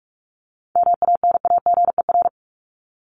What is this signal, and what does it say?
Morse code, heard as a keyed tone, about 44 wpm, 700 Hz. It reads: MANAGER
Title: Top 1501-1750 Words 45wpm